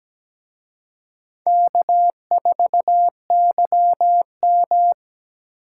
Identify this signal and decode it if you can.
Morse code, heard as a keyed tone, about 20 wpm, 700 Hz. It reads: K4YM
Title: Speed Racing - USA 1x2 Call Signs 17wpm